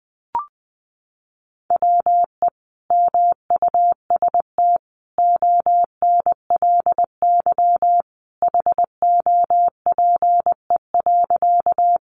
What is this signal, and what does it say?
Morse code, heard as a keyed tone, about 20 wpm, 700 Hz. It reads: WE MUST ONLY HOPE.
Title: Sentences from Top 1000 Words 20wpm